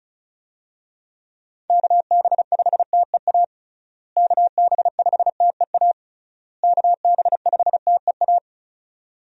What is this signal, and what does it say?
Morse code, heard as a keyed tone, about 36 wpm, 700 Hz. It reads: KB5TEA KB5TEA KB5TEA
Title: USA Call Signs - 6 Characters - Repeated 3x - 35wpm